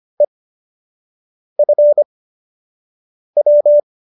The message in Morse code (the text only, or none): EFW